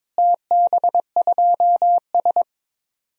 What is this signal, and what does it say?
Morse code, heard as a keyed tone, about 22 wpm, 700 Hz. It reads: TB2S
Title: DXCC-100 Call Signs - 4 Characters - Repeated 3x - 22wpm